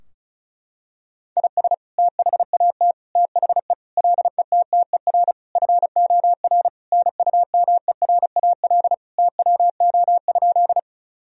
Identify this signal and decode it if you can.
Morse code, heard as a keyed tone, about 34 wpm, 700 Hz. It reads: IS THAT THE LETTER FOR NUMERAL TWO?
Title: Sentences from Top 1000 Words - Review Part 2 - 35wpm